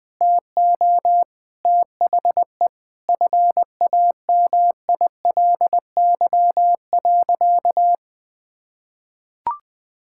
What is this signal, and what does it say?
Morse code, heard as a keyed tone, about 19 wpm, 700 Hz. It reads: TO THE FAMILY.
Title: Sentences from Top 1000 Words - Review Part 6 - 20wpm